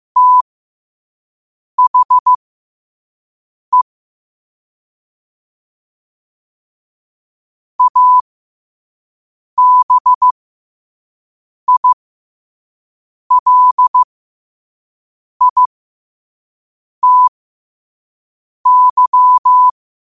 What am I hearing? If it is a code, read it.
Morse code, heard as a keyed tone, about 15 wpm, spaced as for 5 wpm, 1000 Hz. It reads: THE ABILITY
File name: qotdfiles / 5wpm